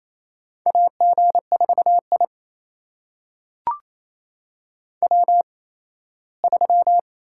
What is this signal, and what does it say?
Morse code, heard as a keyed tone, about 28 wpm, 700 Hz. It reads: A G 4 I W 3
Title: USA Call Signs - 4 Characters - 8x Character Spacing - 28wpm